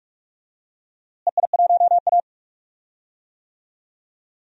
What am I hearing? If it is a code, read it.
Morse code, heard as a keyed tone, about 46 wpm, 700 Hz. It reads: EI1A
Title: DXCC-100 Call Signs - 4 Characters - Repeated 3x - 45wpm